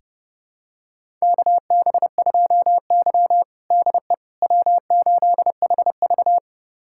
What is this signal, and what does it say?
Morse code, heard as a keyed tone, about 30 wpm, 700 Hz. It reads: KB2Y DE W8HV
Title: USA Callsign DE Callsign - 4 Characters - 30wpm